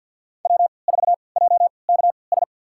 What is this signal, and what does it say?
Morse code, heard as a keyed tone, about 47 wpm, spaced as for 28 wpm, 700 Hz. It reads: W4JKS
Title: USA Call Signs - 5 Characters - 3x Character Spacing - 50wpm